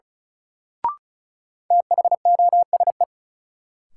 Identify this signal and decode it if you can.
Morse code, heard as a keyed tone, about 35 wpm, 700 Hz. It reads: THOSE
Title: Sentences from Top 300 Words - New Word Drill - 35wpm